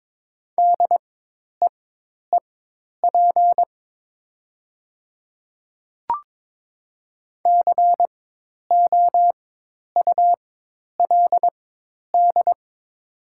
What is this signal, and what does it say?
Morse code, heard as a keyed tone, about 23 wpm, spaced as for 10 wpm, 700 Hz. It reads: DEEP COULD
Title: 4x Character Spacing - Top 500 Words - 22wpm